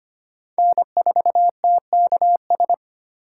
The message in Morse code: N4TKS

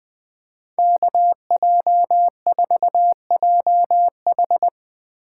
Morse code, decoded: KJ4JH

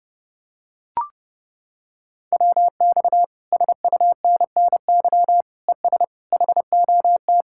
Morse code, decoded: WX SUNNY ES HOT

